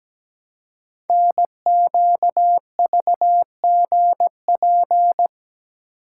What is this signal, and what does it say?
Morse code, heard as a keyed tone, about 17 wpm, 700 Hz. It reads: NQVGP
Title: Code Groups - 5 Letters - 17wpm